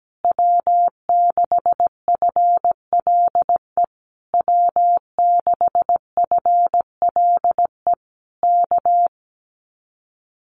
Morse code, heard as a keyed tone, about 17 wpm, 700 Hz. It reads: W6FLE W6FLE K
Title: Common QSO Phrases Encore 17wpm